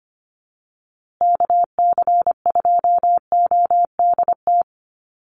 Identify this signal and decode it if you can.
Morse code, heard as a keyed tone, about 25 wpm, 700 Hz. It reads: KC2ODT
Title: USA Call Signs Repeated 2x 25wpm